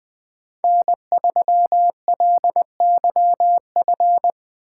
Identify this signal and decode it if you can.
Morse code, heard as a keyed tone, about 20 wpm, 700 Hz. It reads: N3LYF